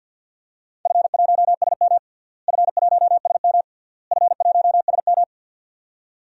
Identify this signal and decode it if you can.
Morse code, heard as a keyed tone, about 45 wpm, 700 Hz. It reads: F1SM F1SM F1SM
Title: DXCC-100 Call Signs - 4 Characters - Repeated 3x - 50wpm